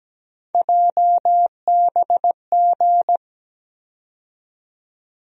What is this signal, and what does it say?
Morse code, heard as a keyed tone, about 17 wpm, 700 Hz. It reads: JBG
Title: USA Call Sign Postfixes - 3 Letters 17wpm